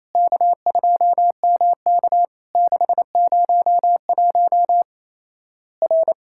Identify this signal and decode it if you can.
Morse code, heard as a keyed tone, about 27 wpm, 700 Hz. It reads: K2MK 601 R
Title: Worked All Europe DX Contest - QTC Exchange - 28wpm